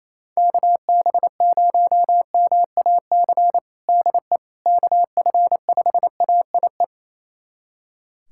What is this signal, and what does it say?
Morse code, heard as a keyed tone, about 27 wpm, 700 Hz. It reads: KB0MAC DE KF5AIE
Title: USA Callsign DE Callsign - 6 Characters - 28wpm